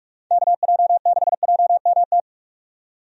Morse code, heard as a keyed tone, about 46 wpm, 700 Hz. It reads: KJ6JMT
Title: USA Call Signs - 6 Characters - 45wpm